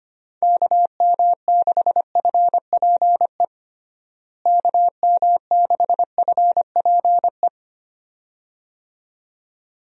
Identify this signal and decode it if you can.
Morse code, heard as a keyed tone, about 25 wpm, 700 Hz. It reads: KM6FPE KM6FPE
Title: USA Call Signs - 6 Characters - Repeated 2x - 25wpm